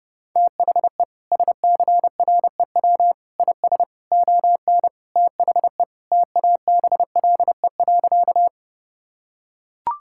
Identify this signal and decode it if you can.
Morse code, heard as a keyed tone, about 32 wpm, 700 Hz. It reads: THE SCREW IS ON THE TABLE.
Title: Sentences from Top 1600 Words 30wpm